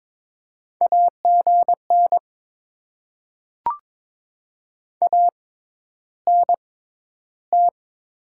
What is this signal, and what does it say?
Morse code, heard as a keyed tone, about 21 wpm, spaced as for 11 wpm, 700 Hz. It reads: AGN ANT
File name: Top 20 QSO Elements - 6x Character Spacing - 22wpm